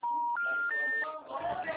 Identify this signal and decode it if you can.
Special information tone, with speech.